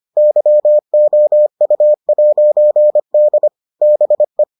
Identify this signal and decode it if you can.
Morse code, heard as a keyed tone, about 25 wpm, 600 Hz. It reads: YOU'D BE